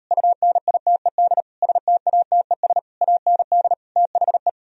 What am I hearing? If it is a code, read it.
Morse code, heard as a keyed tone, about 38 wpm, 700 Hz. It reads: UNITED STATES AND THE